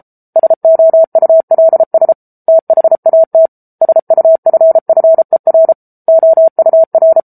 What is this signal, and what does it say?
Morse code, heard as a keyed tone, about 34 wpm, 650 Hz. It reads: SOULS THAT SUFFER OUR
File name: Book Two / 07-Chapter 7 - 33 WPM